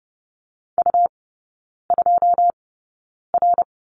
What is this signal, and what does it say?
Morse code, heard as a keyed tone, about 30 wpm, spaced as for 9 wpm, 690 Hz. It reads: U2R
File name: Lesson 17e - Letter u - Three Characters 30wpm